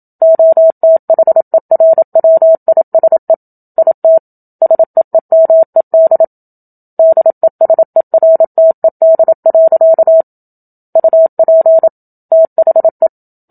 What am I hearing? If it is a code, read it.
Morse code, heard as a keyed tone, about 27 wpm, 650 Hz. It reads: OTHERWISE IT SEEMED DESERTED. UP THE
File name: Book Two / 01-Chapter 1 - 27 WPM